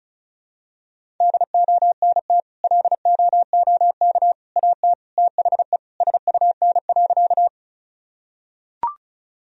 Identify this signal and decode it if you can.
Morse code, heard as a keyed tone, about 36 wpm, 700 Hz. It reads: DONT LOOK AT THE SUN.
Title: Sentences from Top 1000 Words - Review Part 1 - 35wpm